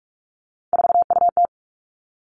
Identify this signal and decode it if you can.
Morse code, heard as a keyed tone, about 46 wpm, 700 Hz. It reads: 4UT
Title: Three Challenging Letter-Number Combinations 45wpm